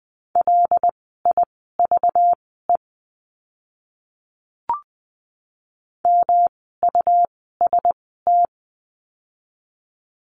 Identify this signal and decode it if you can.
Morse code, heard as a keyed tone, about 20 wpm, spaced as for 14 wpm, 700 Hz. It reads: LIVE MUST
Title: 237 Lesson 38f - Letter q - Words 20wpm